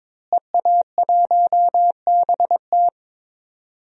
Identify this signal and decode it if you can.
Morse code, heard as a keyed tone, about 22 wpm, 700 Hz. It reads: EA1BT
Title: DXCC-100 Call Signs - 5 Characters - Repeated 2x - 22wpm